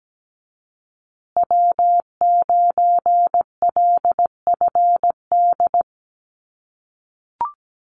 Morse code, heard as a keyed tone, about 17 wpm, 700 Hz. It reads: W9LFD
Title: USA Call Signs Repeated 2x 17wpm